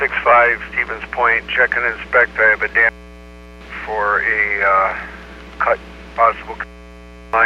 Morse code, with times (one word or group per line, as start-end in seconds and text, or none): none